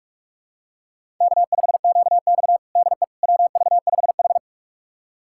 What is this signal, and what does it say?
Morse code, heard as a keyed tone, about 46 wpm, 700 Hz. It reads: K5QX DE WV5H
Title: USA Callsign DE Callsign - 4 Characters - 45wpm